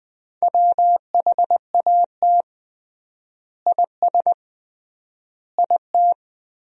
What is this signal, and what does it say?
Morse code, heard as a keyed tone, about 21 wpm, 700 Hz. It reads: WHAT IS IT
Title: 3x Word-Spacing - Sentences from Top 100 Words - 20wpm